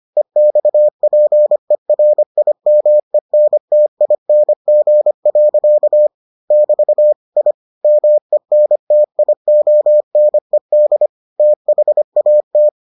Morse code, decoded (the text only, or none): EXPERIMENTING. <BT> I MENTIONED THAT